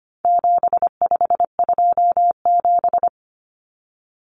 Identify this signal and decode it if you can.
Morse code, heard as a keyed tone, about 25 wpm, 700 Hz. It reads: 7527
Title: Numbers - 4-Digit 25wpm